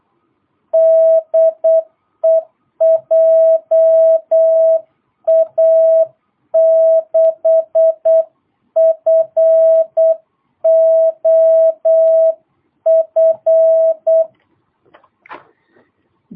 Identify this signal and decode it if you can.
Morse code, heard as a keyed tone, about 8 wpm, 650 Hz. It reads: DE JA6FOF
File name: jh8cbh / 2022cw